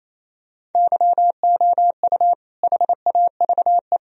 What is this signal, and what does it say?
Morse code, heard as a keyed tone, about 28 wpm, 700 Hz. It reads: YOU HAVE
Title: Sentences from Top 1000 Words - Review Part 1 - 28wpm